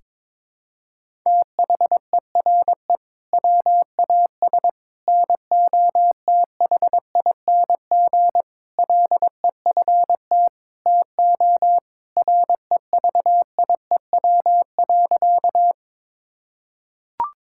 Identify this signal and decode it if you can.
Morse code, heard as a keyed tone, about 22 wpm, 700 Hz. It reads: THERE WAS NOTHING LEFT TO REVIEW.
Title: Sentences from Top 1000 Words - Review Part 6 - 22wpm